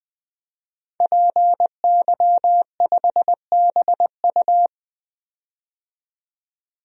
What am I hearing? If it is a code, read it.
Morse code, heard as a keyed tone, about 21 wpm, 700 Hz. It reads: PY5BU
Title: DXCC-100 Call Signs - 5 Characters - 20wpm